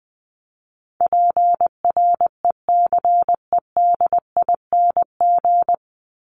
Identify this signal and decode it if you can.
Morse code, heard as a keyed tone, about 20 wpm, 700 Hz. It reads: PRECEDING
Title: ING Suffix - Top 100 Words with Base Words 20wpm